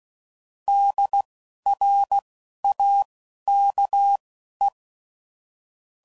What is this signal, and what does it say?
Morse code, heard as a keyed tone, about 16 wpm, spaced as for 11 wpm, 800 Hz. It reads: DRAKE
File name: CW Code Groups @15WPM / QSOs / WD0FBT Spaced